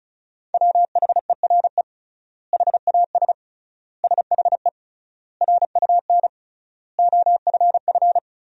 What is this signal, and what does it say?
Morse code, heard as a keyed tone, about 36 wpm, 700 Hz. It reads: WHERE HAS SHE RUN OFF
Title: 3x Word-Spacing - Sentences from Top 300 Words - 35wpm